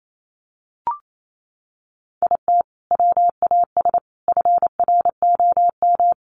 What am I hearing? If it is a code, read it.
Morse code, heard as a keyed tone, about 28 wpm, 700 Hz. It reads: IT WAS FROM
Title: Sentences from Top 100 Words 28wpm